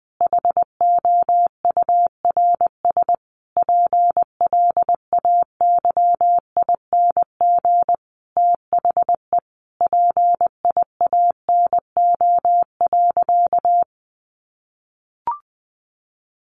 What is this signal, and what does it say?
Morse code, heard as a keyed tone, about 20 wpm, 700 Hz. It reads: HOURS PLAYING THE PIANO.
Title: Additional Easy Phrases 20wpm